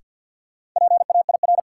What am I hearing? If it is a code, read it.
Morse code, heard as a keyed tone, about 46 wpm, 700 Hz. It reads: PAIR
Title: Sets of 3 Words - New Word Drill - 50wpm